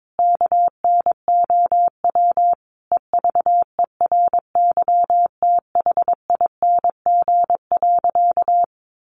Morse code, heard as a keyed tone, about 22 wpm, 700 Hz. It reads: KNOW EVERYTHING.
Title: Sentences from Top 700 Words 22wpm